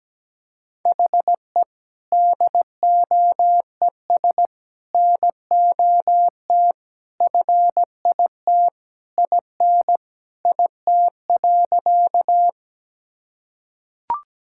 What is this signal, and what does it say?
Morse code, heard as a keyed tone, about 17 wpm, 700 Hz. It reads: HE DOES NOT FIT IN IT.
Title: Sentences from Top 1000 Words - Review Part 5 - 17wpm